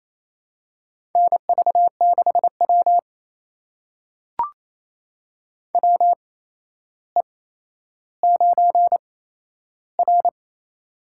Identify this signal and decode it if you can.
Morse code, heard as a keyed tone, about 27 wpm, spaced as for 12 wpm, 700 Hz. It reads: NV6W WE9R
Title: USA Call Signs - 4 Characters - 8x Character Spacing - 28wpm